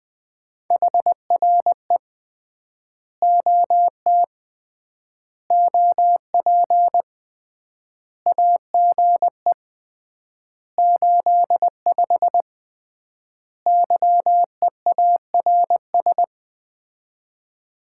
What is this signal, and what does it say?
Morse code, heard as a keyed tone, about 20 wpm, 700 Hz. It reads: HRE OT OP AGE 85 YEARS
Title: Common QSO Phrases - 3x Word-Spacing - 20wpm